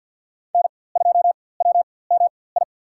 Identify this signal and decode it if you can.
Morse code, heard as a keyed tone, about 46 wpm, spaced as for 23 wpm, 690 Hz. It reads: N2WMI
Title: USA Call Signs - 5 Characters - 4x Character Spacing - 50wpm